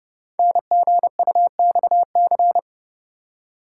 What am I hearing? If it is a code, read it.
Morse code, heard as a keyed tone, about 30 wpm, 700 Hz. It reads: NGUXC
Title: Code Groups - 5 Letter:Numbers - 30wpm